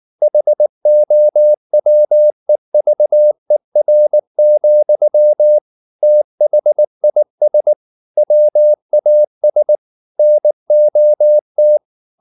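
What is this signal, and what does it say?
Morse code, heard as a keyed tone, about 19 wpm, 600 Hz. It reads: HOWEVER, THIS WAS NOT
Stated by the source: Morse code at 19 wpm from CW med Gnister, reading Ken Follett's novel Never in English.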